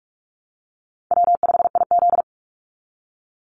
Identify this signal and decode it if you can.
Morse code, heard as a keyed tone, about 44 wpm, 700 Hz. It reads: W5IZ
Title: Call Signs Repeated 3x - 45wpm